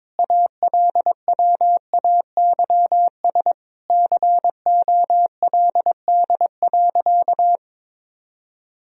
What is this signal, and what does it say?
Morse code, heard as a keyed tone, about 22 wpm, 700 Hz. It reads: ALWAYS COLD.
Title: Sentences from Top 1000 Words - Review Part 1 - 22wpm